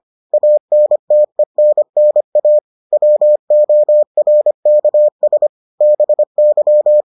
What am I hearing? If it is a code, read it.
Morse code, heard as a keyed tone, about 25 wpm, 600 Hz. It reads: ANTENNA WORKS BY